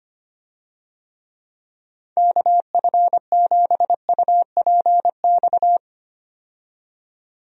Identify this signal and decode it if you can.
Morse code, heard as a keyed tone, about 25 wpm, 700 Hz. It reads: KF7UPX
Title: USA Call Signs - 6 Characters - 25wpm